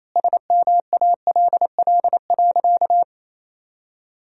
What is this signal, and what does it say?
Morse code, heard as a keyed tone, about 28 wpm, 700 Hz. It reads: SMALL.
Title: 3x Word-Spacing - Sentences from Top 400 Words - 28wpm